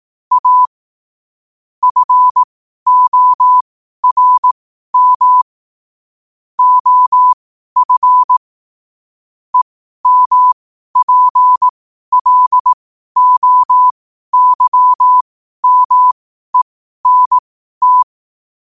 Morse code, heard as a keyed tone, about 18 wpm, spaced as for 12 wpm, 1000 Hz. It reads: A FORM OF EMPLOYMENT
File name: qotdfiles / 12wpm